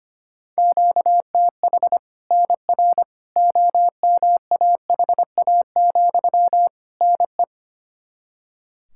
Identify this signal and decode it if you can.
Morse code, heard as a keyed tone, about 25 wpm, 700 Hz. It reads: QTH NR OMAHA, NE
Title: Practice QSO - Exchange 1 - Beginning - 25wpm